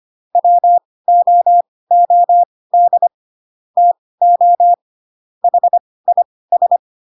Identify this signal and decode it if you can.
Morse code, heard as a keyed tone, about 25 wpm, spaced as for 18 wpm, 700 Hz. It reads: WOOD TO HIS
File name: Pinocchio / Pinocchio ch2 12 25 ik1hge